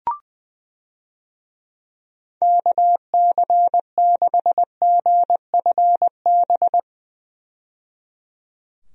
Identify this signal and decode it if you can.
Morse code, heard as a keyed tone, about 20 wpm, 700 Hz. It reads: KC6GFB
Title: USA Call Signs - 6 Characters - 20wpm